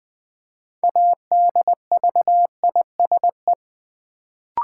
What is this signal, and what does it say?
Morse code, heard as a keyed tone, about 20 wpm, 700 Hz. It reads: ADVISE
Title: Build A Word - 3 Letters At A Time - 6-Letter Words - 20wpm